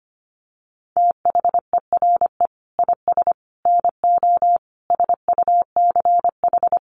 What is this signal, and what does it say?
Morse code, heard as a keyed tone, about 25 wpm, 700 Hz. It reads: THERE IS NO SUCH
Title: Sets of 5 Words Encore 25wpm